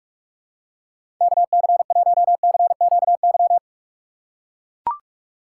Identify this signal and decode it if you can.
Morse code, heard as a keyed tone, about 44 wpm, 700 Hz. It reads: KC1CQY
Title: USA Call Signs - 6 Characters - 4x Character Spacing - 45wpm